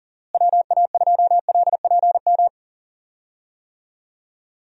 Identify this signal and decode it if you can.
Morse code, heard as a keyed tone, about 38 wpm, 700 Hz. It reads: WA2LPM
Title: USA Call Signs - 6 Characters - Repeated 3x - 40wpm